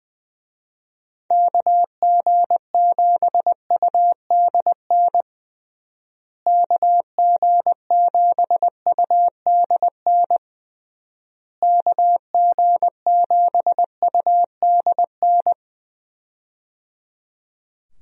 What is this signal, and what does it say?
Morse code, heard as a keyed tone, about 19 wpm, 700 Hz. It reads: KG7UDN KG7UDN KG7UDN
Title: USA Call Signs - 6 Characters - Repeated 3x - 20wpm